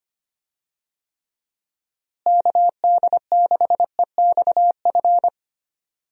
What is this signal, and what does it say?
Morse code, heard as a keyed tone, about 25 wpm, 700 Hz. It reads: KD6EXF
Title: USA Call Signs - 6 Characters - Repeated 3x - 25wpm